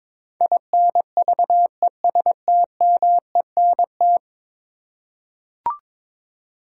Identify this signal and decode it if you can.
Morse code, heard as a keyed tone, about 22 wpm, 700 Hz. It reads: INVESTMENT
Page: morse-code-ninja-practice-22wpm